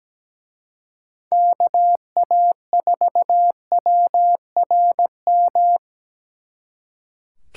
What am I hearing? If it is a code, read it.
Morse code, heard as a keyed tone, about 20 wpm, 700 Hz. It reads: KA4WRM
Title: Speed Racing - USA 2x3 Call Signs 17wpm